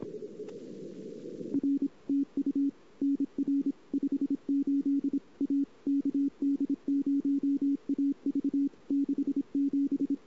Morse code, heard as a keyed tone, about 26 wpm, 280 Hz. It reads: R TU NR 58 A KD0AV 67
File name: CW / 2008-ss-cw-k5tr-op-n5rz -14:00